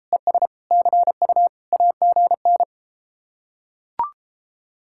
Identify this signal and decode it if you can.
Morse code, heard as a keyed tone, about 33 wpm, 700 Hz. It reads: ES CU AGN